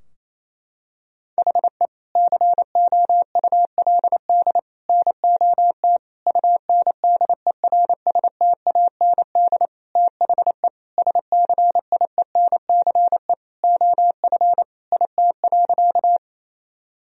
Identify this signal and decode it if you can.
Morse code, heard as a keyed tone, about 28 wpm, 700 Hz. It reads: HE COULD NOT UNDERSTAND THE SCIENCE OF IT.
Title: Sentences from Top 1000 Words - Review Part 2 - 28wpm